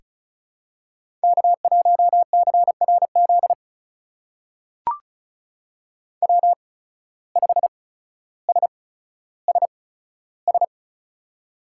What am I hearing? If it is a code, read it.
Morse code, heard as a keyed tone, about 34 wpm, spaced as for 15 wpm, 700 Hz. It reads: K1CRZ W5SSS
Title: USA Call Signs - 5 Characters - 8x Character Spacing - 35wpm